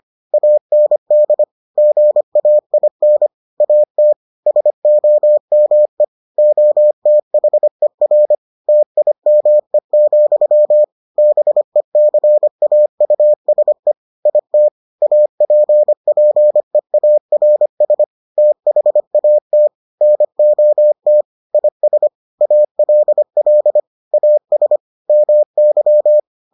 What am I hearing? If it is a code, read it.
Morse code, heard as a keyed tone, about 25 wpm, 600 Hz. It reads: AND GAIN AT SOME OTHER TIME, BECAUSE IT APPEARS THAT NOT IS ALL AS MY